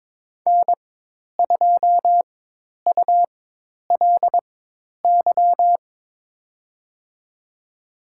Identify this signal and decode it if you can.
Morse code, heard as a keyed tone, about 22 wpm, spaced as for 10 wpm, 700 Hz. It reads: N2ULY
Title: USA Call Signs - 5 Characters - 4x Character Spacing - 22wpm